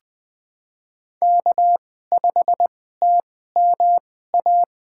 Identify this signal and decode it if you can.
Morse code, heard as a keyed tone, about 20 wpm, spaced as for 14 wpm, 700 Hz. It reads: K5TMA
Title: USA Call Signs - 2x Character Spacing - 20wpm